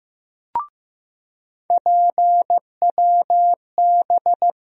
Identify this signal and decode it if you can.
Morse code, heard as a keyed tone, about 16 wpm, 700 Hz. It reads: PWB
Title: USA Call Sign Postfixes - 3 Letters 15wpm